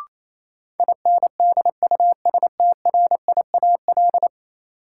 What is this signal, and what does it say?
Morse code, heard as a keyed tone, about 28 wpm, 700 Hz. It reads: INDUSTRIAL
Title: Sets of 5 Words - New Word Drill - 28wpm